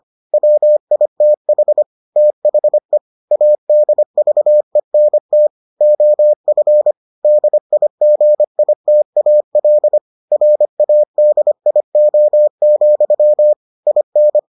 Morse code, heard as a keyed tone, about 25 wpm, 600 Hz. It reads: WITH THE ADVENT OF DIGITAL RADIO, IN